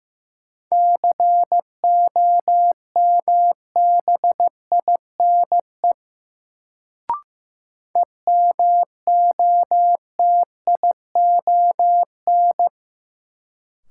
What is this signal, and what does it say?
Morse code, heard as a keyed tone, about 15 wpm, 700 Hz. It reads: COMBINE EMOTION
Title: Sentences from Top 1600 Words - New Word Drill Alt - 15wpm